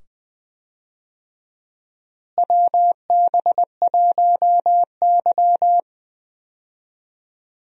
Morse code, heard as a keyed tone, about 20 wpm, 700 Hz. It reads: WB1Y
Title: USA Call Signs - 4 Characters - Repeated 2x - 20wpm